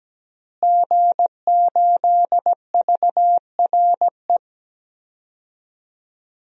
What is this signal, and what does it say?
Morse code, heard as a keyed tone, about 17 wpm, 700 Hz. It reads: G8VRE
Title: DXCC-100 Call Signs - 5 Characters - Repeated 3x - 17wpm